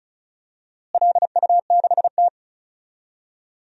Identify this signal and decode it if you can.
Morse code, heard as a keyed tone, about 36 wpm, 700 Hz. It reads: LU6T
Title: DXCC-100 Call Signs - 4 Characters - Repeated 2x - 35wpm